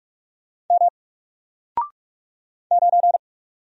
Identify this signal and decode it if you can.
Morse code, heard as a keyed tone, about 44 wpm, 700 Hz. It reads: M 9
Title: Single Letter-Number Alt 45wpm